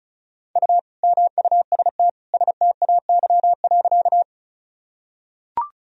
Morse code, heard as a keyed tone, about 36 wpm, 700 Hz. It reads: U MUST STAY.
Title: Sentences from Top 500 Words and Abbreviations 35wpm